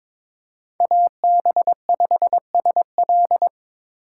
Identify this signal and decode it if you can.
Morse code, heard as a keyed tone, about 22 wpm, 700 Hz. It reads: AB5SL
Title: USA Call Signs - 5 Characters - 2x Character Spacing - 22wpm